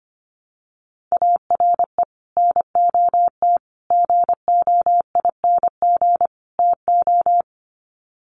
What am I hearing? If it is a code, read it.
Morse code, heard as a keyed tone, about 25 wpm, 700 Hz. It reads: ARE NOT GOING TO
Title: Sets of 4 Words 25wpm